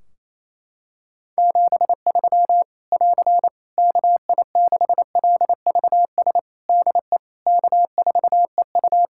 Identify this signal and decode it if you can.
Morse code, heard as a keyed tone, about 28 wpm, 700 Hz. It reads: 73 <AR> KI6LVS DE K4EU
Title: K1USN SST QSO Practice 28wpm